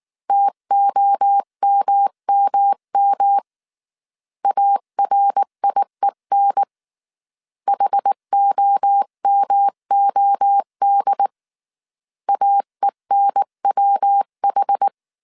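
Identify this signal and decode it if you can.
Morse code, heard as a keyed tone, about 19 wpm, 800 Hz. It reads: TOMMM ARIEN HOMOD AENWH
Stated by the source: Morse code dictation at 18 wpm from the IW3SGT QRS Morse code training course, five-character groups of 300 characters.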